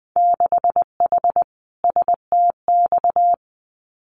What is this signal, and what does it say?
Morse code, heard as a keyed tone, about 20 wpm, 700 Hz. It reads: 6H STX
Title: Winter Field Day Exchanges Encore 20wpm